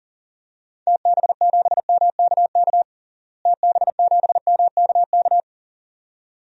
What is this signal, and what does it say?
Morse code, heard as a keyed tone, about 40 wpm, 700 Hz. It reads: TB7MKK TB7MKK
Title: DXCC-100 Call Signs - 6 Characters - Repeated 2x - 40wpm